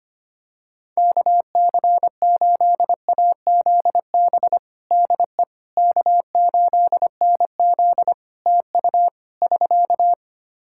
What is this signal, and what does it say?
Morse code, heard as a keyed tone, about 25 wpm, 700 Hz. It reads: KC8AZB DE K8NZ TU <SK>
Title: Practice QSO - Exchange 4 - Ending - 25wpm